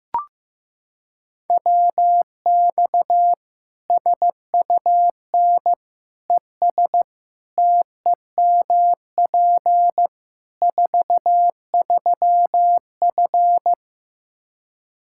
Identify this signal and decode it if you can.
Morse code, heard as a keyed tone, about 15 wpm, 700 Hz. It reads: WX SUN ES TEMP 43F
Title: Practice QSO - Exchange 2 - Info - 15wpm